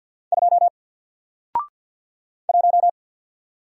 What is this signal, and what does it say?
Morse code, heard as a keyed tone, about 48 wpm, 700 Hz. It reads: 2 1
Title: Single Character (Letter:Number:Punctuation:Prosign) Alt 50wpm